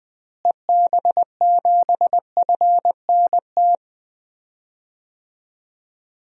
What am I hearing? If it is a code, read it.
Morse code, heard as a keyed tone, about 20 wpm, 700 Hz. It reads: EB7FNT
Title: DXCC-100 Call Signs - 6 Characters - Repeated 2x - 20wpm